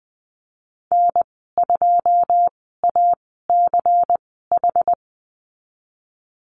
Morse code, heard as a keyed tone, about 20 wpm, spaced as for 15 wpm, 700 Hz. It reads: N2ACH
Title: 140 Lesson 23g - Letter f - Call Signs 20wpm